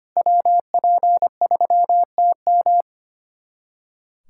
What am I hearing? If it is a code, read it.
Morse code, heard as a keyed tone, about 25 wpm, 700 Hz. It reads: WP3TM
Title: USA Callsign DE Callsign - 5 Characters - 25wpm